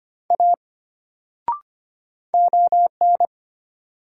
Morse code, heard as a keyed tone, about 25 wpm, 700 Hz. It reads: A ON
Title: Sentences from Top 100 Words - New Word Drill - 25wpm